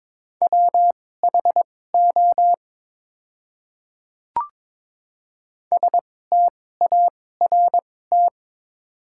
Morse code, heard as a keyed tone, about 21 wpm, spaced as for 16 wpm, 700 Hz. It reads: WHO START
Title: 2x Character Spacing - Top 500 Words - 22wpm